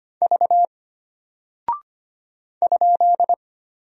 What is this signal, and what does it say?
Morse code, heard as a keyed tone, about 25 wpm, 700 Hz. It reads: V ?
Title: Single Character (Letter:Number:Punctuation:Prosign) Alt 25wpm